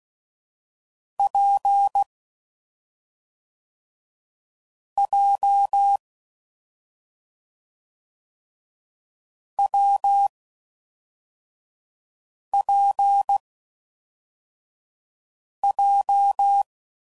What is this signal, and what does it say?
Morse code, heard as a keyed tone, about 16 wpm, 800 Hz. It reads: P J W P J
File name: Combinations / Difficult Combos / AWJ1